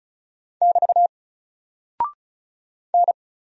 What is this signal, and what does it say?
Morse code, heard as a keyed tone, about 35 wpm, 700 Hz. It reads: <BT> N
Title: Single Character (Letter:Number:Punctuation:Prosign) 35wpm